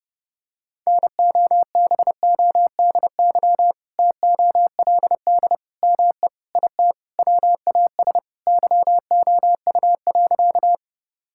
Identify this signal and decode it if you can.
Morse code, heard as a keyed tone, about 30 wpm, 700 Hz. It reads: NOBODY TOLD ME IT WAS YOU.
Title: Sentences from Top 1600 Words 30wpm